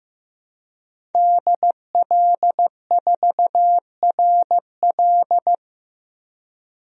Morse code, heard as a keyed tone, about 15 wpm, 700 Hz. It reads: DL4RL
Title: DXCC-100 Call Signs - 5 Characters - 15wpm